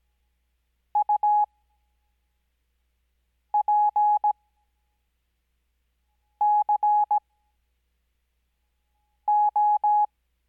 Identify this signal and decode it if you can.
Morse code, heard as a keyed tone, about 17 wpm, spaced as for 4 wpm, 840 Hz. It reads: UPCO
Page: Farnsworth Spacing: 15WPM Character speed Spaced at 5WPM.